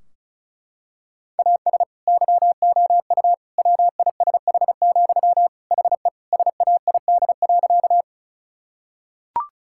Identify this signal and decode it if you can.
Morse code, heard as a keyed tone, about 34 wpm, 700 Hz. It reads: AS YOU WISH, HE SAID.
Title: Sentences from Top 1600 Words 35wpm